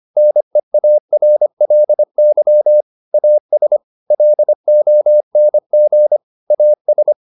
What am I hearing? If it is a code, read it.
Morse code, heard as a keyed tone, about 25 wpm, 600 Hz. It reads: NEARLY AS LONG AS